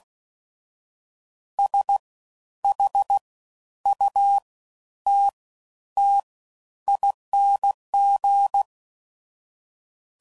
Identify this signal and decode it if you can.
Morse code, heard as a keyed tone, about 16 wpm, spaced as for 10 wpm, 800 Hz. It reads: SHUTTING
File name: Suffixes / Special INGs / Fast Ings